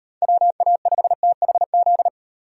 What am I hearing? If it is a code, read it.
Morse code, heard as a keyed tone, about 38 wpm, 700 Hz. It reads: WA5THZ